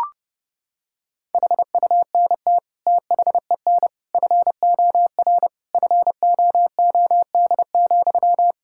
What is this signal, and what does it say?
Morse code, heard as a keyed tone, about 30 wpm, 700 Hz. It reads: HUNT THEN FOR FOOD,
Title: Sentences from Top 1600 Words 30wpm